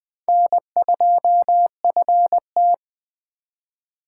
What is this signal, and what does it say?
Morse code, heard as a keyed tone, about 19 wpm, 700 Hz. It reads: N2FT